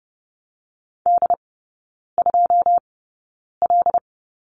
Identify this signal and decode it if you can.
Morse code, heard as a keyed tone, about 30 wpm, spaced as for 9 wpm, 700 Hz. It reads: D2L
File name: 081 Lesson 14e - Number 2 - Three Characters 30wpm